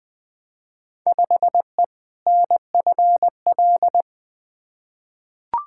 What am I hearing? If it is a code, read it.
Morse code, heard as a keyed tone, about 20 wpm, 700 Hz. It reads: 5E NFL
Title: Field Day Exchanges Encore - 20wpm